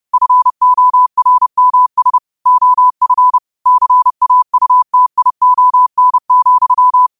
Morse code, heard as a keyed tone, about 30 wpm, 1000 Hz. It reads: FORMS OF CAUTION,
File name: qotdfiles / 30wpm